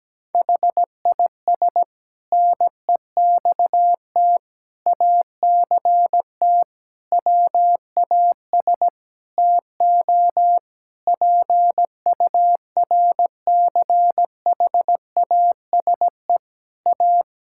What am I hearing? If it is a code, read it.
Morse code, heard as a keyed tone, about 17 wpm, 700 Hz. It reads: HIS NEXT ACT WAS TO PURCHASE A
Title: Sentences from Top 1000 Words - Review Part 5 - 17wpm